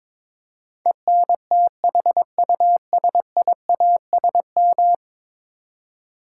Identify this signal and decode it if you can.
Morse code, heard as a keyed tone, about 22 wpm, 700 Hz. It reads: ENTHUSIASM
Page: morse-code-ninja-practice-22wpm